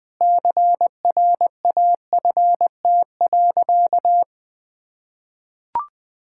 Morse code, heard as a keyed tone, about 20 wpm, 700 Hz. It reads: CRAFT.
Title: Sentences from Top 1000 Words - Review Part 6 - 20wpm